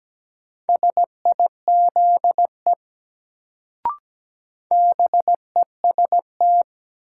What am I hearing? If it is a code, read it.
Morse code, heard as a keyed tone, about 17 wpm, 700 Hz. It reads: SIZE BEST
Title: Sentences from Top 400 Words - New Word Drill - 17wpm